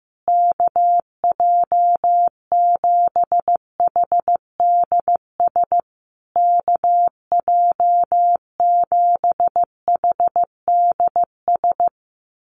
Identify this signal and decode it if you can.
Morse code, heard as a keyed tone, about 15 wpm, 700 Hz. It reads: KJ7HDS KJ7HDS
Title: USA Call Signs Repeated 3x 15wpm